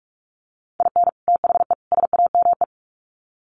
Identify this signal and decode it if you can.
Morse code, heard as a keyed tone, about 44 wpm, 700 Hz. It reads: IN THE SAME
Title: Sets of 3 Words 45wpm